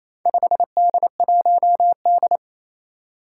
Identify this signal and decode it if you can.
Morse code, heard as a keyed tone, about 28 wpm, 700 Hz. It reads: 5D1D
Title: Four Challenging Letter-Number Combinations 28wpm